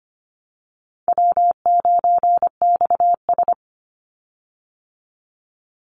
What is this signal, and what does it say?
Morse code, heard as a keyed tone, about 25 wpm, 700 Hz. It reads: W9XS
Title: Easy Call Signs Repeated 2x - 25wpm